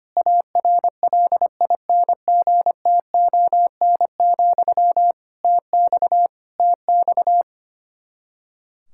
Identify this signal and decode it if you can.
Morse code, heard as a keyed tone, about 25 wpm, 700 Hz. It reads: ARLINGTON, TX TX
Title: Practice QSO - Exchange 1 - Beginning - 25wpm